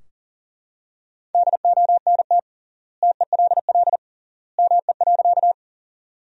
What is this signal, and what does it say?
Morse code, heard as a keyed tone, about 40 wpm, 700 Hz. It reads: DONT TELL ME.
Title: 3x Word-Spacing - Sentences from Top 300 Words - 40wpm